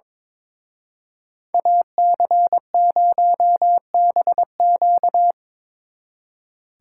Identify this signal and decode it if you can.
Morse code, heard as a keyed tone, about 26 wpm, 700 Hz. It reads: AC0BQ
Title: Speed Racing - USA Call Signs 22wpm